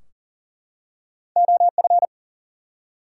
Morse code, heard as a keyed tone, about 40 wpm, 700 Hz. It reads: OF
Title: Two Letter Words - 40wpm